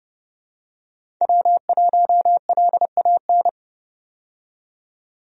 Morse code, heard as a keyed tone, about 32 wpm, 700 Hz. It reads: W1LAN
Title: USA Call Signs - 5 Characters - 30wpm